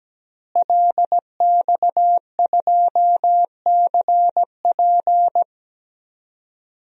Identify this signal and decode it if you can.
Morse code, heard as a keyed tone, about 17 wpm, 700 Hz. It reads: LX2CP
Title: DXCC-100 Call Signs - 5 Characters - Repeated 3x - 17wpm